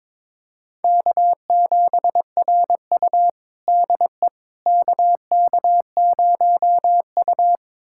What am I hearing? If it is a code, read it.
Morse code, heard as a keyed tone, about 22 wpm, 700 Hz. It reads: K7RU DE KK0U
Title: USA Callsign DE Callsign - 4 Characters - 22wpm